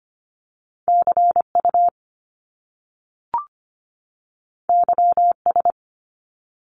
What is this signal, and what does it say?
Morse code, heard as a keyed tone, about 25 wpm, 700 Hz. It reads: CU YS
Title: Top Two Letter Combinations in Words 25wpm